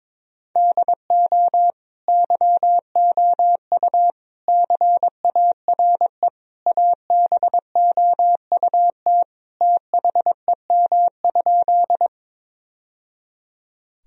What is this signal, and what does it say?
Morse code, heard as a keyed tone, about 22 wpm, 700 Hz. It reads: DO YOU CARE ABOUT THEM?
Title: Sentences from Top 1000 Words - Review Part 1 - 22wpm